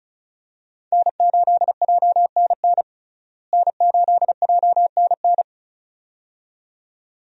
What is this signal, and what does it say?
Morse code, heard as a keyed tone, about 34 wpm, 700 Hz. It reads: N8JNN N8JNN
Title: USA Call Signs - 5 Characters - Repeated 2x - 35wpm